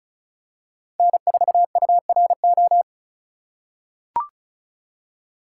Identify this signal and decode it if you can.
Morse code, heard as a keyed tone, about 34 wpm, 700 Hz. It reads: N4URO
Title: USA Call Signs - 5 Characters - 2x Character Spacing - 35wpm